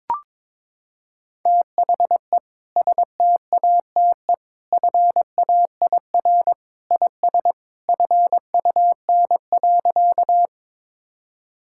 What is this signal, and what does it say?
Morse code, heard as a keyed tone, about 22 wpm, 700 Hz. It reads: THE STATE FAIR IS FUN.
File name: practice / 22wpm / Sentences with 2-Word Phrases - 22wpm